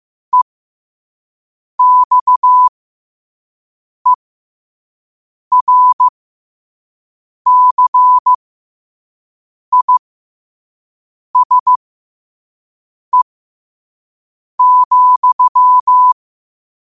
Morse code, heard as a keyed tone, about 15 wpm, spaced as for 5 wpm, 1000 Hz. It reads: EXERCISE,
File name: 5wpm